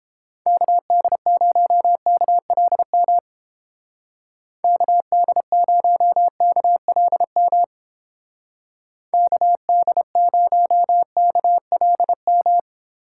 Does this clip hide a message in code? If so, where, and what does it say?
Yes: Morse code from 0.5 to 12.6 s, heard as a keyed tone, about 30 wpm, 700 Hz. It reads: KD0KLM KD0KLM KD0KLM